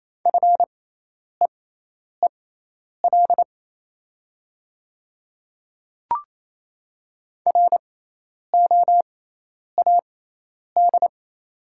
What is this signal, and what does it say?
Morse code, heard as a keyed tone, about 28 wpm, spaced as for 10 wpm, 700 Hz. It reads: FEEL ROAD
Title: 6x Character Spacing - Top 500 Words - 28wpm